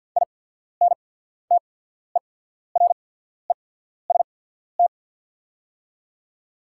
Morse code, heard as a keyed tone, about 48 wpm, spaced as for 14 wpm, 690 Hz. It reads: INTEREST
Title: 8x Character Spacing - Top 500 Words - 50wpm